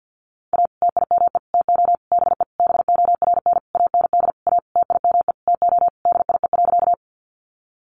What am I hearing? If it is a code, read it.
Morse code, heard as a keyed tone, about 50 wpm, 700 Hz. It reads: A TIME TO BE BORN AND A TIME TO DIE.
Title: Sentences from Top 1300 Words 50wpm